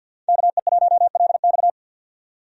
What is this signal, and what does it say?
Morse code, heard as a keyed tone, about 51 wpm, 700 Hz. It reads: KE1LX
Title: USA Call Signs - 5 Characters - 50wpm